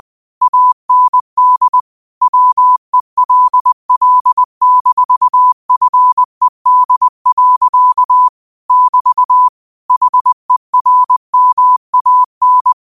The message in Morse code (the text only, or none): AND WELL-FED. <BT> HERMAN